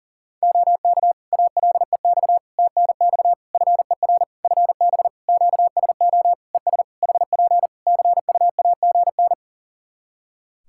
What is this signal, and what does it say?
Morse code, heard as a keyed tone, about 40 wpm, 700 Hz. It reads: OK ALEX TNX FER FB QSO ES HP CUAGN
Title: Practice QSO - Exchange 4 - Ending - 40wpm